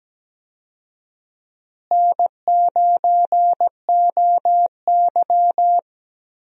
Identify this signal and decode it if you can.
Morse code, heard as a keyed tone, about 17 wpm, 700 Hz. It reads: N9OY
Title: USA Call Signs - 4 Characters - Repeated 2x - 17wpm